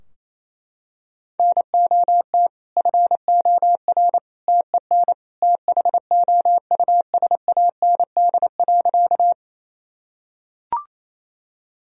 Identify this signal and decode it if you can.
Morse code, heard as a keyed tone, about 28 wpm, 700 Hz. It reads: NOT FOR TEN THOUSAND.
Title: Sentences from Top 1000 Words - Review Part 3 - 28wpm